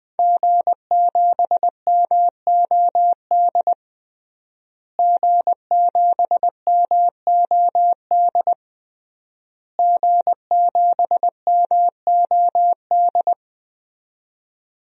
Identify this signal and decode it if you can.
Morse code, heard as a keyed tone, about 20 wpm, 700 Hz. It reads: G7MOD G7MOD G7MOD
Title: DXCC-100 Call Signs - 5 Characters - Repeated 3x - 20wpm